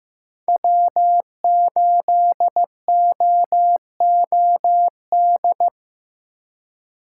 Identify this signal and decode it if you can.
Morse code, heard as a keyed tone, about 15 wpm, 700 Hz. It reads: W8OOD
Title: USA Call Signs - 5 Characters - 4x Character Spacing - 15wpm